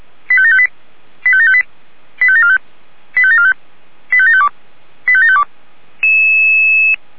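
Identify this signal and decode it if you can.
Five-tone selective call (ZVEI), heard as quick runs of five tones, tones of about 70 ms. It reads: 85758 85758 85745 85745 85752 85752
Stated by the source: ZVEI five-tone selective call 85726 received by SDR.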